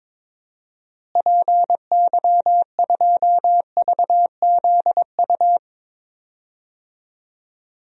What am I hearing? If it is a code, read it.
Morse code, heard as a keyed tone, about 22 wpm, 700 Hz. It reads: PY2VZU
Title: DXCC-100 Call Signs - 6 Characters - Repeated 3x - 22wpm